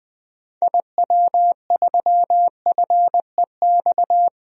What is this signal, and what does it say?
Morse code, heard as a keyed tone, about 20 wpm, 700 Hz. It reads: IW3FEX
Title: DXCC-100 Call Signs - 6 Characters - 20wpm